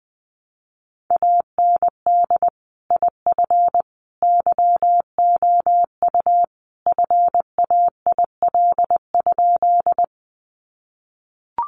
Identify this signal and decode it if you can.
Morse code, heard as a keyed tone, about 19 wpm, 700 Hz. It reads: AND IF YOU FAIL?
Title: Sentences from Top 1100 Words 20wpm